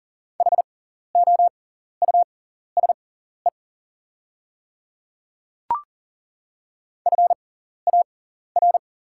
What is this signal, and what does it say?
Morse code, heard as a keyed tone, about 40 wpm, spaced as for 14 wpm, 700 Hz. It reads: HOUSE FAR
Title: 6x Character Spacing - Top 300 Words - 40wpm